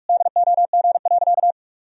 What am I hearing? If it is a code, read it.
Morse code, heard as a keyed tone, about 41 wpm, 700 Hz. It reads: DOG.
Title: Sentences from Top 1000 Words - Review Part 2 - 45wpm